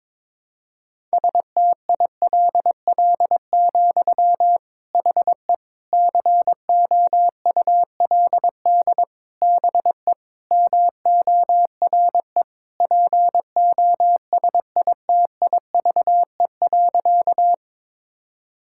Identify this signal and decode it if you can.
Morse code, heard as a keyed tone, about 22 wpm, 700 Hz. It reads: STILL, HE COULD BE MORE POSITIVE.
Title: Sentences from Top 1000 Words - Review Part 5 - 22wpm